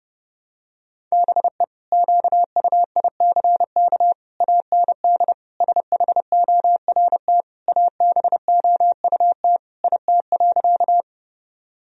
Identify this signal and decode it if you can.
Morse code, heard as a keyed tone, about 30 wpm, 700 Hz. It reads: BE QUICK AND SHORT ABOUT IT.
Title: Sentences from Top 1000 Words - Review Part 2 - 30wpm